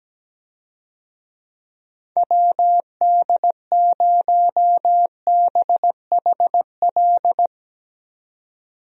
Morse code, heard as a keyed tone, about 17 wpm, 700 Hz. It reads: WD0BHL
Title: USA Call Signs - 6 Characters - 17wpm